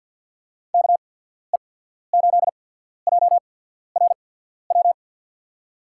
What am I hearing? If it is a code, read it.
Morse code, heard as a keyed tone, about 54 wpm, spaced as for 14 wpm, 700 Hz. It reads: KE8JRW
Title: USA Call Signs - 6 Characters - 8x Character Spacing - 50wpm